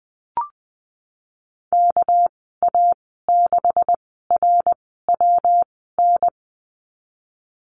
Morse code, heard as a keyed tone, about 20 wpm, spaced as for 15 wpm, 700 Hz. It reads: KA6RWN
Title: 231 Lesson 37g - Letter x - Call Signs 20wpm